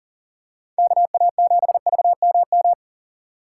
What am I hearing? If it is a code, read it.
Morse code, heard as a keyed tone, about 40 wpm, 700 Hz. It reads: KA7VMM
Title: USA Call Signs - 6 Characters - 4x Character Spacing - 40wpm